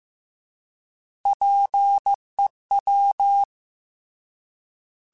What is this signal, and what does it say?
Morse code, heard as a keyed tone, about 15 wpm, 800 Hz. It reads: PEW
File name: Alphabetical / Alphabetical mp3s / Ps @13